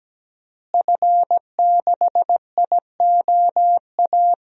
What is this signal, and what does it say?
Morse code, heard as a keyed tone, about 17 wpm, 700 Hz. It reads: F6IOA
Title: DXCC-100 Call Signs - 5 Characters - Repeated 3x - 17wpm